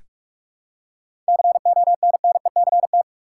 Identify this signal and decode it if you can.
Morse code, heard as a keyed tone, about 47 wpm, 700 Hz. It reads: CONNECT
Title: Seven Letter Words - 45wpm